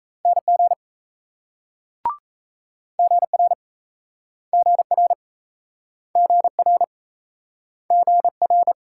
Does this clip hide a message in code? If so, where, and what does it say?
Yes: Morse code from 0.3 to 8.7 s, heard as a keyed tone, about 38 wpm, 700 Hz. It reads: NG GR GR GR GR